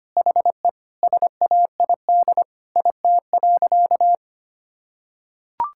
Morse code, heard as a keyed tone, about 25 wpm, 700 Hz. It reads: HE SAID IT.
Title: Sentences from Top 1600 Words 25wpm